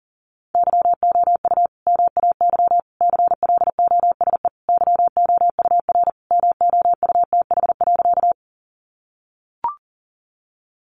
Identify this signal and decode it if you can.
Morse code, heard as a keyed tone, about 40 wpm, 700 Hz. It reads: YOU MAY CLOSE YOUR MOUTH.
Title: Sentences from Top 1500 Words 40wpm